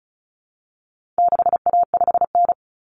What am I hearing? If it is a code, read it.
Morse code, heard as a keyed tone, about 34 wpm, 700 Hz. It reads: 6A5N
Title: Common Cut Numbers - 4-Digit 35wpm